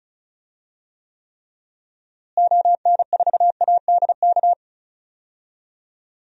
Morse code, heard as a keyed tone, about 36 wpm, 700 Hz. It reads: ON4ADK
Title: DXCC-100 Call Signs - 6 Characters - Repeated 3x - 35wpm